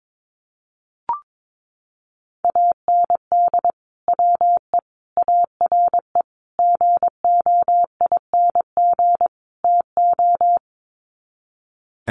Morse code, heard as a keyed tone, about 22 wpm, 700 Hz. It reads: AND WE ARE GOING TO
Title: Sets of 5 Words Encore2 22wpm